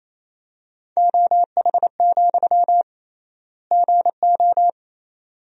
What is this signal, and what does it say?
Morse code, heard as a keyed tone, about 28 wpm, 700 Hz. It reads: OH, GO
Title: 3x Word-Spacing - Sentences from Top 500 Words - 28wpm